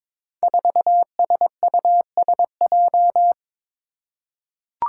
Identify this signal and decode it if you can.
Morse code, heard as a keyed tone, about 22 wpm, 700 Hz. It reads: 4SUSJ
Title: Five Challenging Letter-Number Combinations 22wpm